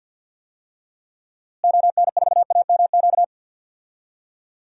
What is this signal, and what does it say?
Morse code, heard as a keyed tone, about 50 wpm, 700 Hz. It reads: ON4AMQ
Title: DXCC-100 Call Signs - 6 Characters - Repeated 2x - 50wpm